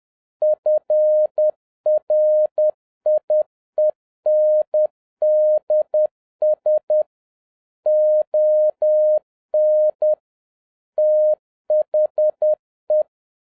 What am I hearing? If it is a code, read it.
Morse code, heard as a keyed tone, about 10 wpm, 610 Hz. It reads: FRIENDS ON THE
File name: 2011-07-mp3 / n36 THE 3.11 2 10wpm